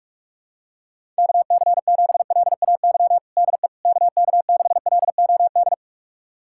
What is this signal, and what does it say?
Morse code, heard as a keyed tone, about 41 wpm, 700 Hz. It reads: KC7LAY DE KK6LOD
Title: USA Callsign DE Callsign - 6 Characters - 45wpm